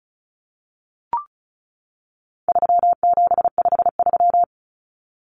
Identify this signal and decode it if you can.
Morse code, heard as a keyed tone, about 34 wpm, 700 Hz. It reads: 3753
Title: Common Cut Numbers - 4-Digit 35wpm